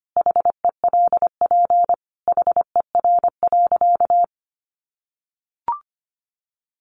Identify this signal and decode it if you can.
Morse code, heard as a keyed tone, about 25 wpm, 700 Hz. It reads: HELP HER.
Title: Sentences from Top 1000 Words 25wpm